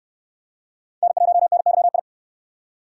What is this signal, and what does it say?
Morse code, heard as a keyed tone, about 69 wpm, 700 Hz. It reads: N1NYI